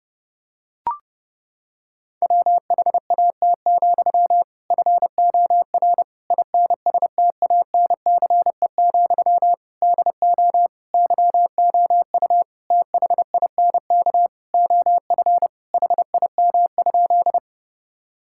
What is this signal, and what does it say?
Morse code, heard as a keyed tone, about 32 wpm, 700 Hz. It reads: WHAT, FOR INSTANCE, DO YOU THINK OF HIM?
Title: Sentences from Top 1000 Words - Review Part 5 - 30wpm